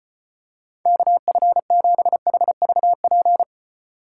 Morse code, heard as a keyed tone, about 29 wpm, 700 Hz. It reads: KF7HVP
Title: Speed Racing - USA 2x3 Call Signs 25wpm